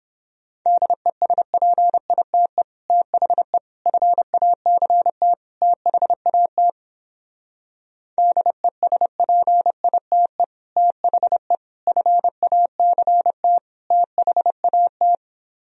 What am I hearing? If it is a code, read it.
Morse code, heard as a keyed tone, about 29 wpm, 700 Hz. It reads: DESPITE THE FACT THAT DESPITE THE FACT THAT